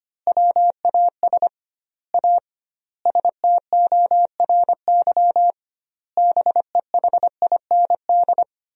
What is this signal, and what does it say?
Morse code, heard as a keyed tone, about 25 wpm, 700 Hz. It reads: WAS A STORY BEHIND
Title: 2x Word-Spacing - Sentences from Top 500 Words - 25wpm